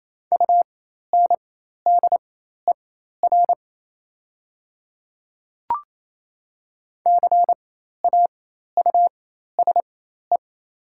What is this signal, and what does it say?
Morse code, heard as a keyed tone, about 28 wpm, spaced as for 13 wpm, 700 Hz. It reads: UNDER CAUSE
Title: 4x Character Spacing - Top 200 Words - 28wpm